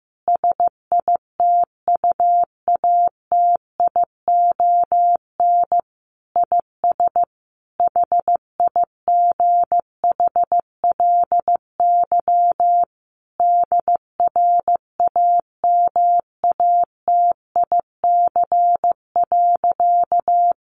Morse code, heard as a keyed tone, about 15 wpm, 700 Hz. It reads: SITUATION IS HIGHLY DRAMATIC.
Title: Sentences from Top 1300 Words 15wpm